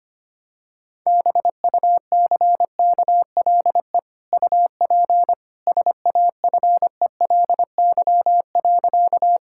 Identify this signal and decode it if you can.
Morse code, heard as a keyed tone, about 25 wpm, 700 Hz. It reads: BUCKLE UP SAFELY.
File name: Easy Sentences with Hints - 25wpm